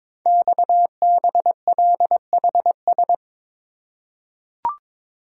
Morse code, heard as a keyed tone, about 22 wpm, 700 Hz. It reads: XBLHS
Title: Code Groups - 5 Letters - 22wpm